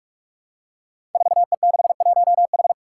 Morse code, heard as a keyed tone, about 46 wpm, 700 Hz. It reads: 4E61H
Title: Five Challenging Letter-Number Combinations 45wpm